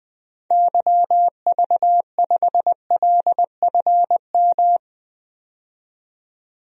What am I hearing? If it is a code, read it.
Morse code, heard as a keyed tone, about 20 wpm, 700 Hz. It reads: YV5LFM
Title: DXCC-100 Call Signs - 6 Characters - 20wpm